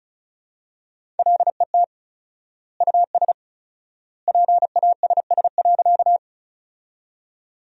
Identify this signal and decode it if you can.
Morse code, heard as a keyed tone, about 36 wpm, 700 Hz. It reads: LET US PASS.
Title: 4x Word-Spacing - Sentences from Top 400 Words - 35wpm